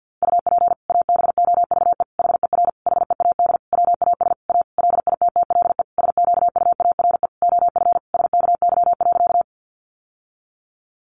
Morse code, heard as a keyed tone, about 50 wpm, 700 Hz. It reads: UP ABOVE HER HEAD WAS A LITTLE SQUARE OF SKY.
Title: Sentences from Top 1200 Words 50wpm